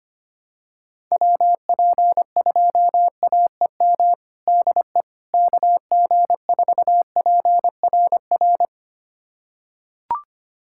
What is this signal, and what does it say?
Morse code, heard as a keyed tone, about 25 wpm, 700 Hz. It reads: WP2AEM DE KG4PRR
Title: USA Callsign DE Callsign - 6 Characters - 25wpm